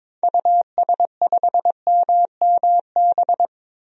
Morse code, heard as a keyed tone, about 22 wpm, 700 Hz. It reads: US5MMB
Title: DXCC-100 Call Signs - 6 Characters - 22wpm